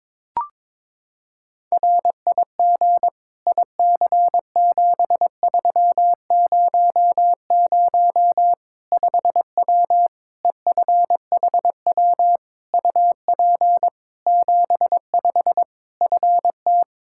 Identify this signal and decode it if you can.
Morse code, heard as a keyed tone, about 22 wpm, 700 Hz. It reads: RIG IC7300 5W EFHW UP 75 FT
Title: Practice QSO - Exchange 2 - Info - 22wpm